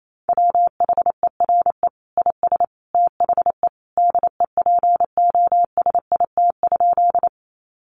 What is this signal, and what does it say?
Morse code, heard as a keyed tone, about 28 wpm, 700 Hz. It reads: WHERE IS THE DEPOSIT?
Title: Sentences from Top 1300 Words 28wpm